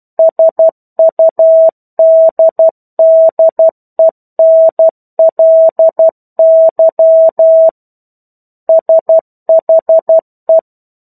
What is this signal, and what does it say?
Morse code, heard as a keyed tone, about 12 wpm, 650 Hz. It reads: SUDDENLY SHE
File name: 01-Chapter-01 12 WPM